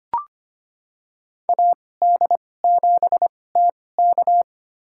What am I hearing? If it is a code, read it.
Morse code, heard as a keyed tone, about 25 wpm, spaced as for 18 wpm, 700 Hz. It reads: AD7TK
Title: USA Call Signs - 5 Characters - 2x Character Spacing - 25wpm